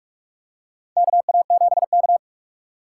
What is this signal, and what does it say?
Morse code, heard as a keyed tone, about 44 wpm, 700 Hz. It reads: KA7K
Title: USA Call Signs - 4 Characters - 4x Character Spacing - 45wpm